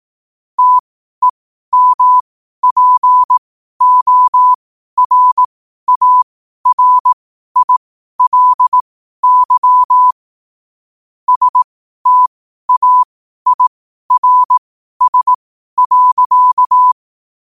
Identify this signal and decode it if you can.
Morse code, heard as a keyed tone, about 18 wpm, spaced as for 12 wpm, 1000 Hz. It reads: TEMPORARILY STAIRS.
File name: qotdfiles / 12wpm